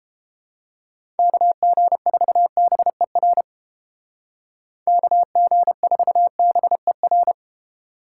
Morse code, heard as a keyed tone, about 29 wpm, 700 Hz. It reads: KG4BER KG4BER